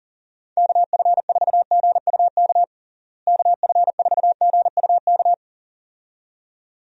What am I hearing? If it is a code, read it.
Morse code, heard as a keyed tone, about 37 wpm, 700 Hz. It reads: KF4GUK KF4GUK
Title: USA Call Signs - 6 Characters - Repeated 2x - 40wpm